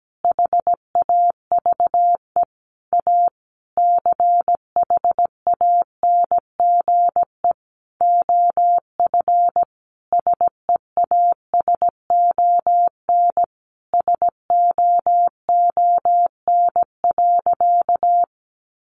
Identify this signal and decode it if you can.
Morse code, heard as a keyed tone, about 17 wpm, 700 Hz. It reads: HAVE A CHANGE OF SEASON SOON.
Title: Sentences from Top 1400 Words 17wpm